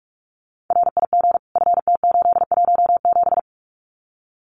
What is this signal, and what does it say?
Morse code, heard as a keyed tone, about 46 wpm, 700 Hz. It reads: RIG FT817
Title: Common QSO Phrases Encore 45wpm